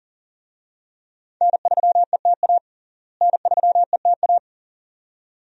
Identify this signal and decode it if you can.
Morse code, heard as a keyed tone, about 40 wpm, 700 Hz. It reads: N3ETA N3ETA
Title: USA Call Signs - 5 Characters - Repeated 2x - 40wpm